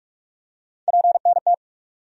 Morse code, heard as a keyed tone, about 45 wpm, 700 Hz. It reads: PNT
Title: USA Call Sign Postfixes - 3 Letters 45wpm